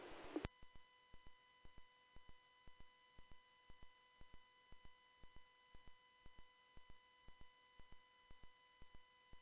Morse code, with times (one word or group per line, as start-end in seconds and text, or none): none